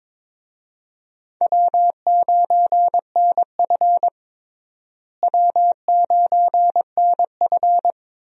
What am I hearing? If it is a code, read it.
Morse code, heard as a keyed tone, about 22 wpm, 700 Hz. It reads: W9NF W9NF
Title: USA Call Signs - 4 Characters - Repeated 2x - 22wpm